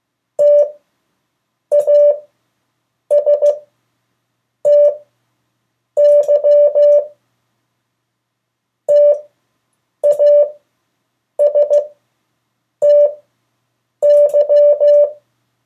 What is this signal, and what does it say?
Morse code, heard as a keyed tone, about 15 wpm, spaced as for 7 wpm, 580 Hz. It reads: TASTY TASTY
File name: 3of4-words-2